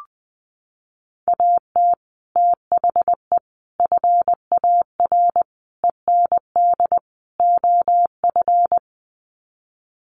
Morse code, heard as a keyed tone, about 20 wpm, 700 Hz. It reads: AT THE FAR END OF
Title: Sets of 5 Words 20wpm